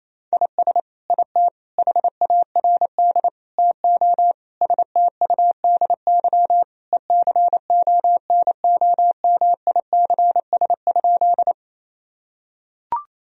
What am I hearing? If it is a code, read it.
Morse code, heard as a keyed tone, about 28 wpm, 700 Hz. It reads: IS IT HARD TO STUDY ECONOMICS?
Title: Sentences from Top 1000 Words - Review Part 3 - 28wpm